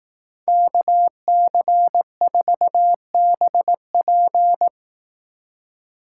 Morse code, heard as a keyed tone, about 18 wpm, 700 Hz. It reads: KC4BP
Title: Speed Racing - USA 2x2 Call Signs 15wpm